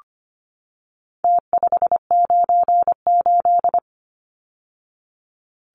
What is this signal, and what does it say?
Morse code, heard as a keyed tone, about 25 wpm, 700 Hz. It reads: T598
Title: Numbers - Contest Style - 4-Digit 25wpm